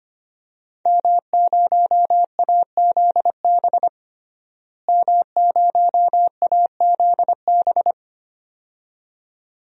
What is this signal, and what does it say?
Morse code, heard as a keyed tone, about 25 wpm, 700 Hz. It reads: M0AZB M0AZB
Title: DXCC-100 Call Signs - 5 Characters - Repeated 2x - 25wpm